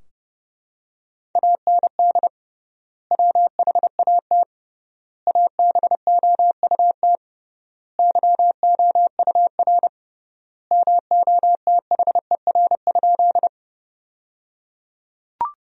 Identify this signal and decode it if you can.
Morse code, heard as a keyed tone, about 30 wpm, 700 Hz. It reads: AND WHAT ABOUT YOUR MOTHER?
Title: 3x Word-Spacing - Sentences from Top 200 Words - 30wpm